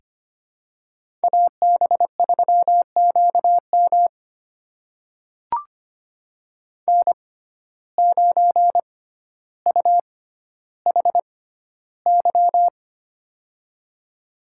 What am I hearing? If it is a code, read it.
Morse code, heard as a keyed tone, about 25 wpm, spaced as for 13 wpm, 700 Hz. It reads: AB3QM N9UHY
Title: USA Call Signs - 5 Characters - 6x Character Spacing - 25wpm